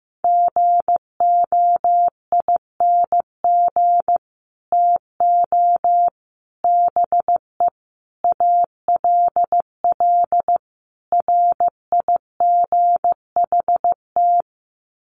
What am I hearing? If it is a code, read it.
Morse code, heard as a keyed tone, about 15 wpm, 700 Hz. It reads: GOING TO BE ALL RIGHT
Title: Sets of 5 Words Encore2 15wpm